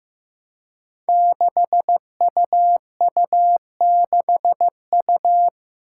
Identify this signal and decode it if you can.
Morse code, heard as a keyed tone, about 15 wpm, 700 Hz. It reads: 6UU6U
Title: Five Challenging Letter-Number Combinations 15wpm